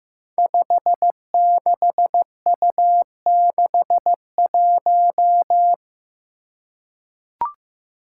Morse code, heard as a keyed tone, about 15 wpm, 700 Hz. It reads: 56U61
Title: Five Challenging Letter-Number Combinations 15wpm